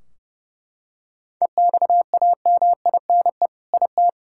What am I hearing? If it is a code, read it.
Morse code, heard as a keyed tone, about 28 wpm, 700 Hz. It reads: EXAMINE IT
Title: Sentences from Top 1600 Words 30wpm